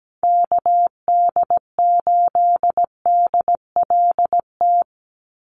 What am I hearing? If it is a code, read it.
Morse code, heard as a keyed tone, about 17 wpm, 700 Hz. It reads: KD8DLT
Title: USA Call Signs 17wpm